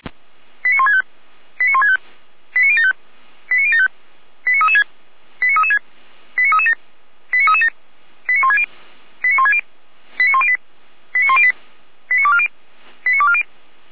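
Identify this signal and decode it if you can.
Five-tone selective call (ZVEI), heard as quick runs of five tones, tones of about 70 ms. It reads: 89165 89165 89075 89075 89336 89336 89337 89337 89160 89160 89108 89108 89240 89240
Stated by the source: ZVEI five-tone call received by SDR.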